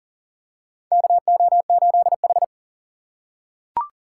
Morse code, heard as a keyed tone, about 37 wpm, 700 Hz. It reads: KO8H
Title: USA Call Signs - 4 Characters - 4x Character Spacing - 40wpm